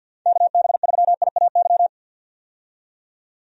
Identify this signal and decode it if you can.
Morse code, heard as a keyed tone, about 53 wpm, 700 Hz. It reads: KB3IAY
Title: USA Call Signs - 6 Characters - 50wpm